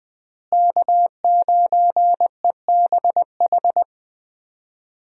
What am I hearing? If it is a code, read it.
Morse code, heard as a keyed tone, about 20 wpm, 700 Hz. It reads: K9EBH
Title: USA Call Signs - 5 Characters - Repeated 2x - 20wpm